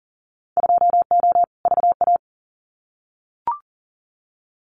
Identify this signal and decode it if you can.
Morse code, heard as a keyed tone, about 38 wpm, 700 Hz. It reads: 2O VA
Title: Winter Field Day Exchanges 40wpm